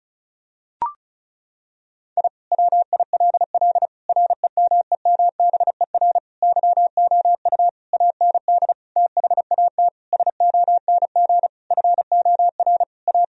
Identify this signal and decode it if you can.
Morse code, heard as a keyed tone, about 36 wpm, 700 Hz. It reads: I WILL REMEMBER YOU AND THAT SONG FOR A
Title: Sentences from Top 1000 Words - Review Part 2 - 35wpm